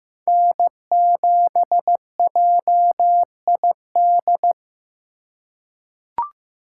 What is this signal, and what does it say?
Morse code, heard as a keyed tone, about 15 wpm, 700 Hz. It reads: N7JID
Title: USA Call Signs - 5 Characters - 2x Character Spacing - 15wpm